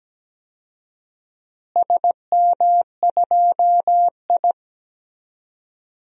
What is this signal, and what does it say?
Morse code, heard as a keyed tone, about 17 wpm, 700 Hz. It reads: SM2I
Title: DXCC-100 Call Signs - 4 Characters - Repeated 3x - 17wpm